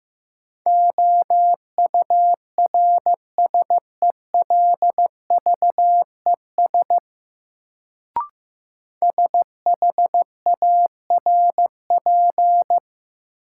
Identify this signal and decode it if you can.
Morse code, heard as a keyed tone, about 15 wpm, 700 Hz. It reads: OURSELVES SHARP
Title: Sentences from Top 1600 Words - New Word Drill Alt - 15wpm